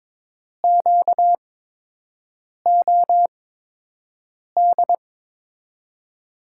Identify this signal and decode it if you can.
Morse code, heard as a keyed tone, about 22 wpm, spaced as for 6 wpm, 700 Hz. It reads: QOD
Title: USA Call Signs - 6 Characters - 8x Character Spacing - 22wpm